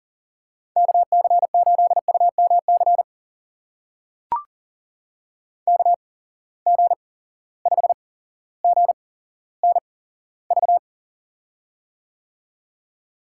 Morse code, heard as a keyed tone, about 39 wpm, spaced as for 17 wpm, 700 Hz. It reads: KC8UMC KG5GNV
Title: USA Call Signs - 6 Characters - 8x Character Spacing - 40wpm